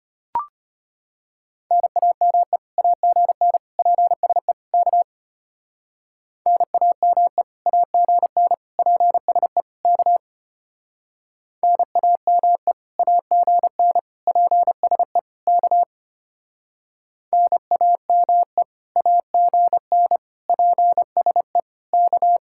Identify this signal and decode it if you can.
Morse code, heard as a keyed tone, about 29 wpm, 700 Hz. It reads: NAME AGN PSE K NAME AGN PSE K NAME AGN PSE K NAME AGN PSE K
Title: Speed-Racing Common QSO Phrases - 25wpm